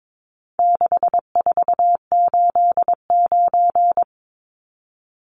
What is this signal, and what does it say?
Morse code, heard as a keyed tone, about 22 wpm, 700 Hz. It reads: 6489
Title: Numbers - 4-Digit 22wpm